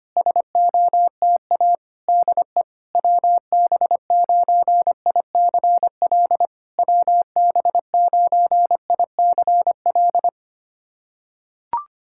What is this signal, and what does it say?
Morse code, heard as a keyed tone, about 25 wpm, 700 Hz. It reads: SOTA DE WB9ICL WB9ICL
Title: SOTA QSO Practice 25wpm